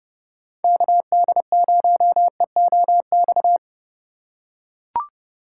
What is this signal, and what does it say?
Morse code, heard as a keyed tone, about 28 wpm, 700 Hz. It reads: KD0EOX